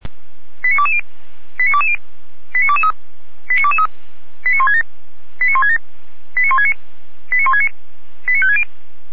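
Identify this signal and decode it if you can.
Five-tone selective call (ZVEI), heard as quick runs of five tones, tones of about 70 ms. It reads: 89220 89220 89223 89223 89167 89167 89169 89169 89560
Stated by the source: ZVEI five-tone call received on SDR.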